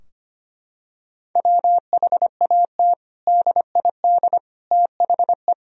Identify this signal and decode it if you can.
Morse code, heard as a keyed tone, about 25 wpm, 700 Hz. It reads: WHAT DID THE
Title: Sentences from Top 1000 Words - Review Part 4 - 25wpm